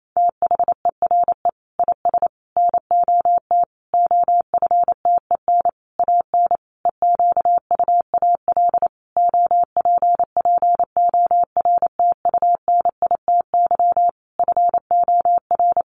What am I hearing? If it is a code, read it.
Morse code, heard as a keyed tone, about 28 wpm, 700 Hz. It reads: THERE IS NOT OFTEN AN EQUAL OPPORTUNITY FOR
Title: Sentences from Top 1400 Words 28wpm